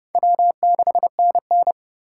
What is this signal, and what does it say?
Morse code, heard as a keyed tone, about 30 wpm, 700 Hz. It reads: W6NN